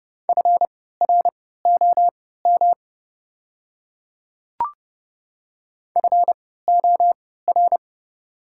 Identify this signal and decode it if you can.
Morse code, heard as a keyed tone, about 30 wpm, spaced as for 17 wpm, 690 Hz. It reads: FROM FOR
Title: 3x Character Spacing - Top 100 Words - 30wpm